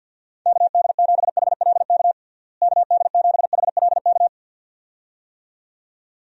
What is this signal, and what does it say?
Morse code, heard as a keyed tone, about 53 wpm, 700 Hz. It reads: KD7HLK KD7HLK
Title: USA Call Signs - 6 Characters - Repeated 2x - 50wpm